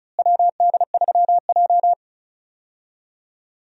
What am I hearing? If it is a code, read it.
Morse code, heard as a keyed tone, about 36 wpm, 700 Hz. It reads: WD3J
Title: USA Call Signs - 4 Characters - Repeated 2x - 35wpm